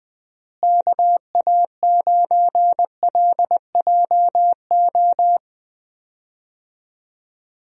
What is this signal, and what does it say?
Morse code, heard as a keyed tone, about 20 wpm, 700 Hz. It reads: KA9LJO
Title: USA Call Signs - 6 Characters - Repeated 2x - 20wpm